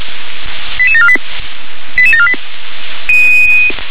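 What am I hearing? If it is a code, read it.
Five-tone selective call (ZVEI), heard as quick runs of five tones, tones of about 70 ms. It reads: 88748 88748
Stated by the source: Five-tone ZVEI selective call 88856, received via SDR.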